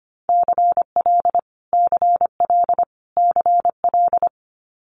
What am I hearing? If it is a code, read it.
Morse code, heard as a keyed tone, about 25 wpm, 700 Hz. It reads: CL CL CL
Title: Multi-letter Phonemes Repeated 3x 25wpm